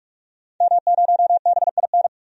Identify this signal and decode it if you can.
Morse code, heard as a keyed tone, about 47 wpm, 700 Hz. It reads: M0BIN
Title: DXCC-100 Call Signs - 5 Characters - 45wpm